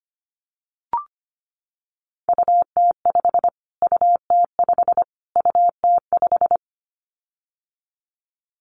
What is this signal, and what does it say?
Morse code, heard as a keyed tone, about 25 wpm, 700 Hz. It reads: UT5 UT5 UT5
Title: Callsign Prefixes for Top 25 DXCC - Repeated 3x - 25wpm